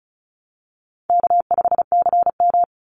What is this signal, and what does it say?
Morse code, heard as a keyed tone, about 34 wpm, 700 Hz. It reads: K5CM
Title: Easy Call Signs Repeated 2x - 35wpm